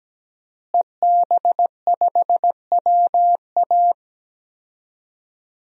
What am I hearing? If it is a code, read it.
Morse code, heard as a keyed tone, about 17 wpm, 700 Hz. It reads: EB5WA
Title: DXCC-100 Call Signs - 5 Characters - Repeated 3x - 17wpm